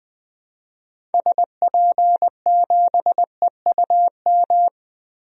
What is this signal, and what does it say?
Morse code, heard as a keyed tone, about 20 wpm, 700 Hz. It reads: SP7EUM
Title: DXCC-100 Call Signs - 6 Characters - Repeated 2x - 20wpm